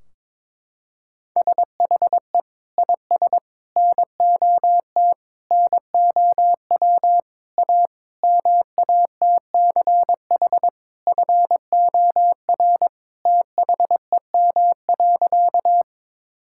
Morse code, heard as a keyed tone, about 22 wpm, 700 Hz. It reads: SHE IS NOT NOW A MATCH FOR THEM.
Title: Sentences from Top 1600 Words 22wpm